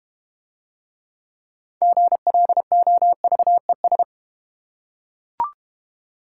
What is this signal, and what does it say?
Morse code, heard as a keyed tone, about 31 wpm, 700 Hz. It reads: GLOVES
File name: Easy Words - Speed-Racing with Hints - 25wpm